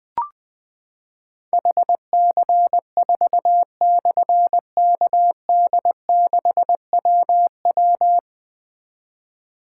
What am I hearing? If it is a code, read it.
Morse code, heard as a keyed tone, about 20 wpm, 700 Hz. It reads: HC4/KD6WW
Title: Difficult Call Signs 20wpm